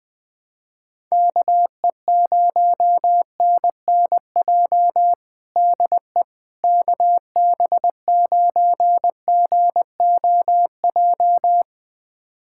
Morse code, heard as a keyed tone, about 20 wpm, 700 Hz. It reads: KE0NNJ DE KB9GOJ
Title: USA Callsign DE Callsign - 6 Characters - 20wpm